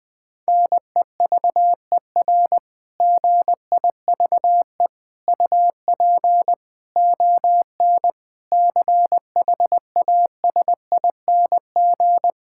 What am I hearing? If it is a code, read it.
Morse code, heard as a keyed tone, about 20 wpm, 700 Hz. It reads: NEVER GIVE UP ON CHASING
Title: Difficult Sentences with Hints - 20wpm